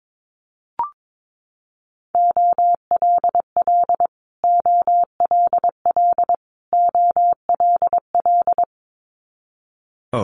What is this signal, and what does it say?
Morse code, heard as a keyed tone, about 22 wpm, 700 Hz. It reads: OLL OLL OLL
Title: Multi-letter Phonemes Repeated 3x 22wpm